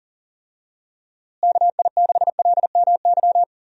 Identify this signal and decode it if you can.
Morse code, heard as a keyed tone, about 40 wpm, 700 Hz. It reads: KI6LMY
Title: USA Call Signs - 6 Characters - Repeated 3x - 40wpm